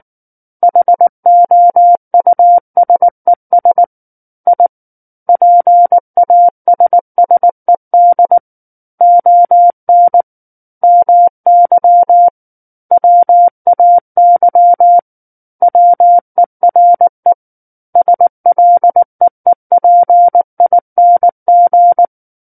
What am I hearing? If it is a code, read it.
Morse code, heard as a keyed tone, about 19 wpm, 700 Hz. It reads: HOUSES I PASSED ON MY WAY WERE SLEEPING
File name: Book One / 10-Chapter 10 - 19 WPM